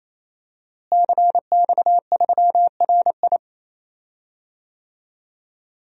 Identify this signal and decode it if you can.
Morse code, heard as a keyed tone, about 28 wpm, 700 Hz. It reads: CX3RI
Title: DXCC-100 Call Signs - 5 Characters - Repeated 3x - 28wpm